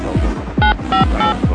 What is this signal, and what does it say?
DTMF, heard as telephone touch tones, tones of about 120 ms. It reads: 958